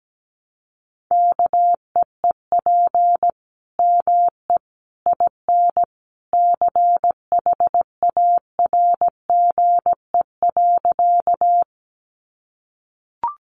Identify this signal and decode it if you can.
Morse code, heard as a keyed tone, about 17 wpm, 700 Hz. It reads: KEEP ME IN CHARGE.
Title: Sentences from Top 900 Words 17wpm